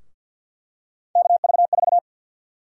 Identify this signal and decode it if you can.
Morse code, heard as a keyed tone, about 53 wpm, 700 Hz. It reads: KV4
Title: USA Call Sign Prefixes - 50wpm